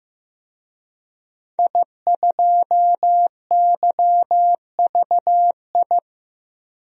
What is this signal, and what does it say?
Morse code, heard as a keyed tone, about 16 wpm, 700 Hz. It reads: I2YVI
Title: DXCC-100 Call Signs - 5 Characters - Repeated 3x - 15wpm